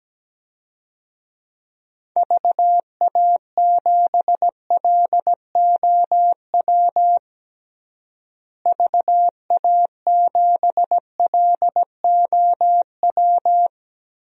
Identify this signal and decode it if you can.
Morse code, heard as a keyed tone, about 17 wpm, 700 Hz. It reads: VA7LOW VA7LOW
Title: DXCC-100 Call Signs - 6 Characters - Repeated 3x - 17wpm